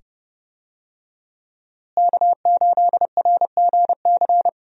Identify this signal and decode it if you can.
Morse code, heard as a keyed tone, about 30 wpm, 700 Hz. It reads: K8RGC
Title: USA Call Signs - 5 Characters - Repeated 2x - 30wpm